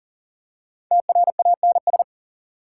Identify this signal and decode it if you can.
Morse code, heard as a keyed tone, about 40 wpm, 700 Hz. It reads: TRANS
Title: Prefixes and Suffixes 40wpm